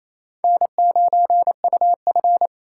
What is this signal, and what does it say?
Morse code, heard as a keyed tone, about 28 wpm, 700 Hz. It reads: N9UF
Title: SOTA QSO Practice 28wpm